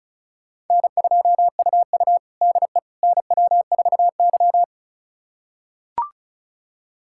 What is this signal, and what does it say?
Morse code, heard as a keyed tone, about 34 wpm, 700 Hz. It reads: N2UU DE NW4Y
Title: USA Callsign DE Callsign - 4 Characters - 35wpm